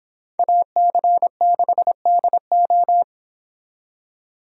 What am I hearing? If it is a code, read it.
Morse code, heard as a keyed tone, about 30 wpm, 700 Hz. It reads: AC6DO